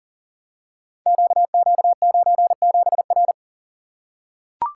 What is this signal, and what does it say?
Morse code, heard as a keyed tone, about 38 wpm, 690 Hz. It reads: QQ97R
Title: Code Groups - 5 Letter:Numbers - 40wpm